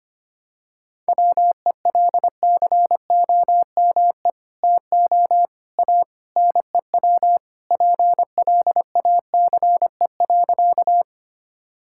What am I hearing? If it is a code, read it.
Morse code, heard as a keyed tone, about 25 wpm, 700 Hz. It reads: WELCOME TO A NEW PLACE.
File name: practice / 25wpm / Sentences with 2-Word Phrases - 25wpm